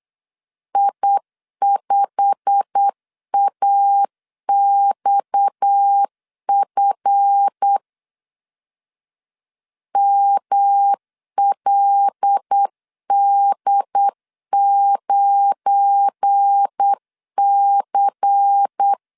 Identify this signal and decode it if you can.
Morse code, heard as a keyed tone, about 8 wpm, 800 Hz. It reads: I5AXF MLD9C
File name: LEZIONI / 9 286 8wpm 3425